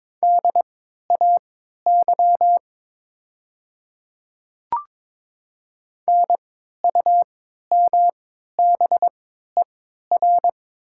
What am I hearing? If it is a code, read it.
Morse code, heard as a keyed tone, about 22 wpm, spaced as for 12 wpm, 700 Hz. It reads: DAY NUMBER
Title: 3x Character Spacing - Top 100 Words - 22wpm